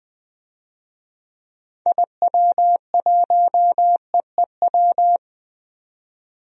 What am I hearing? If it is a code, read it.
Morse code, heard as a keyed tone, about 21 wpm, 700 Hz. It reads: IW1EEW
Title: DXCC-100 Call Signs - 6 Characters - 20wpm